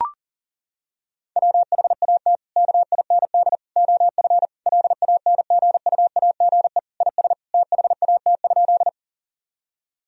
Morse code, heard as a keyed tone, about 40 wpm, 700 Hz. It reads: WHAT KIND OF LANGUAGE IS THAT?
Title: Sentences from Top 1000 Words - Review Part 3 - 40wpm